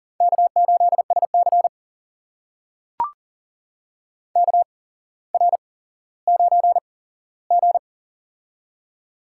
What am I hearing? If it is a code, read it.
Morse code, heard as a keyed tone, about 40 wpm, spaced as for 17 wpm, 690 Hz. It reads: K8SC KR9G
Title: USA Call Signs - 4 Characters - 8x Character Spacing - 40wpm